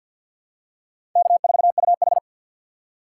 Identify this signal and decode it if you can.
Morse code, heard as a keyed tone, about 48 wpm, 700 Hz. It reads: K4UH
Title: USA Call Signs - 4 Characters - 6x Character Spacing - 50wpm